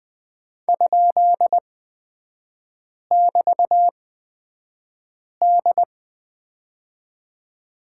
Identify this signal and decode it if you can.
Morse code, heard as a keyed tone, about 20 wpm, spaced as for 5 wpm, 700 Hz. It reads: ?<BT>D
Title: 236 Lesson 38e - Letter q - Three Characters 20wpm